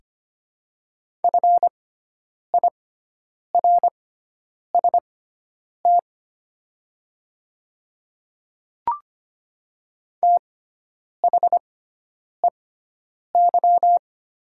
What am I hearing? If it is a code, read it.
Morse code, heard as a keyed tone, about 25 wpm, spaced as for 9 wpm, 700 Hz. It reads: FIRST THEY
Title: 6x Character Spacing - Top 100 Words - 25wpm